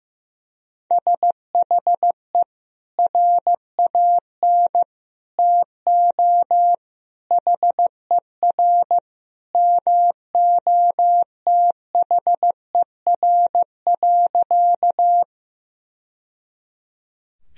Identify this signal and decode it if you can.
Morse code, heard as a keyed tone, about 15 wpm, 700 Hz. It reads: SHE RAN TO HER MOTHER.
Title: Sentences from Top 1000 Words - Review Part 3 - 15wpm